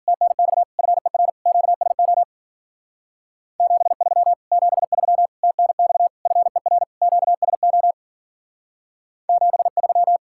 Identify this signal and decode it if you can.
Morse code, heard as a keyed tone, about 50 wpm, 700 Hz. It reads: TNX FER QSO 73 73 TNX FER QSO 73
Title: Speed-Racing Common QSO Phrases - 40wpm